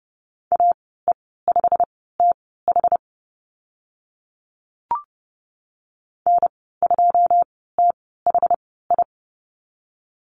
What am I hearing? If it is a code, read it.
Morse code, heard as a keyed tone, about 30 wpm, spaced as for 17 wpm, 700 Hz. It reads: AE5TH N2THI
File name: Lesson 15g - Number 5 - Call Signs 30wpm